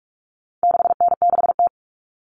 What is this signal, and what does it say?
Morse code, heard as a keyed tone, about 44 wpm, 700 Hz. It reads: 6N6T
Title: All Cut Numbers - 4-Digit 45wpm